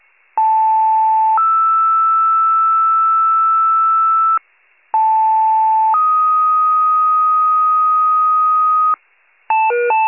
Two-tone page, heard as a one-second tone, then a three-second tone, about 880 then 1320 Hz, about 880 then 1250 Hz. It is heard from 0.4 to 8.9 s.